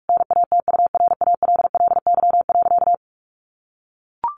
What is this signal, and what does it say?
Morse code, heard as a keyed tone, about 44 wpm, 700 Hz. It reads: NATURALLY.
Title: Sentences from Top 1500 Words 45wpm